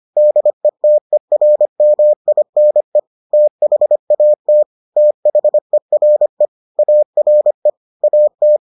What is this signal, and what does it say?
Morse code, heard as a keyed tone, about 25 wpm, 600 Hz. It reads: DETERMINE THAT THERE ARE AT